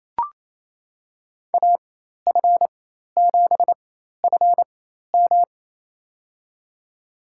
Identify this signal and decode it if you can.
Morse code, heard as a keyed tone, about 28 wpm, spaced as for 13 wpm, 700 Hz. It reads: AF7FM
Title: USA Call Signs - 4x Character Spacing - 28wpm